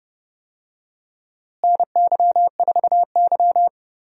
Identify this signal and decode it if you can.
Morse code, heard as a keyed tone, about 30 wpm, 700 Hz. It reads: NY4Y
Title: USA Call Signs - 4 Characters - Repeated 3x - 30wpm